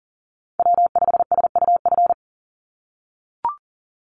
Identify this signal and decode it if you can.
Morse code, heard as a keyed tone, about 41 wpm, 700 Hz. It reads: W5SUF
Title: USA Call Signs 40wpm